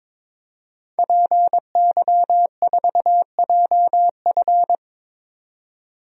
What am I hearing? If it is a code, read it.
Morse code, heard as a keyed tone, about 22 wpm, 700 Hz. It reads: PY4JF
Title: DXCC-100 Call Signs - 5 Characters - 22wpm